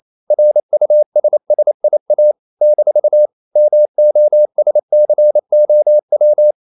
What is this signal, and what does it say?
Morse code, heard as a keyed tone, about 28 wpm, 600 Hz. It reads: RUSSIA - MOSCOW
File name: CW / Practice / NationalCapitals28